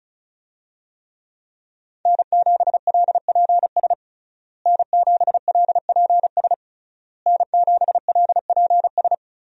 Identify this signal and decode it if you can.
Morse code, heard as a keyed tone, about 36 wpm, 700 Hz. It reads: N7LPS N7LPS N7LPS
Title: USA Call Signs - 5 Characters - Repeated 3x - 35wpm